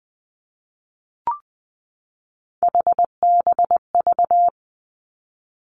Morse code, heard as a keyed tone, about 20 wpm, 700 Hz. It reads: HBV
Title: Three Challenging Letter-Number Combinations 20wpm